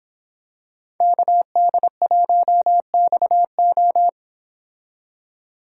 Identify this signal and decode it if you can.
Morse code, heard as a keyed tone, about 26 wpm, 700 Hz. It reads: KD1XO